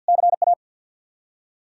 Morse code, heard as a keyed tone, about 49 wpm, 700 Hz. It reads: CA
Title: Practice QSO - Exchange 1 - Beginning - 50wpm